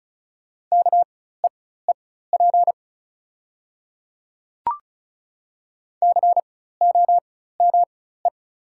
Morse code, heard as a keyed tone, about 36 wpm, spaced as for 16 wpm, 690 Hz. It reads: KEEP COME
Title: 4x Character Spacing - Top 300 Words - 35wpm